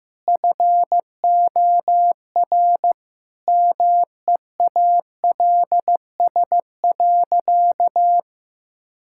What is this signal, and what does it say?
Morse code, heard as a keyed tone, about 15 wpm, 700 Hz. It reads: FOR MEALS.
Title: Pangram Challenge v2 - 15wpm